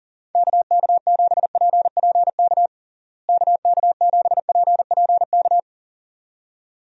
Morse code, heard as a keyed tone, about 40 wpm, 700 Hz. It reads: KK7PPK KK7PPK
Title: USA Call Signs - 6 Characters - Repeated 2x - 40wpm